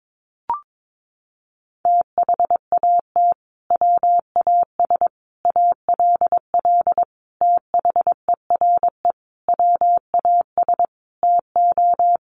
Morse code, heard as a keyed tone, about 22 wpm, 700 Hz. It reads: THAT WAS ALL THERE WAS TO
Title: Sentences from Top 1100 Words 22wpm